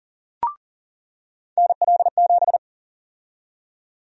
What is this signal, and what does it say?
Morse code, heard as a keyed tone, about 40 wpm, 700 Hz. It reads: NL7
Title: USA Call Sign Prefixes - 40wpm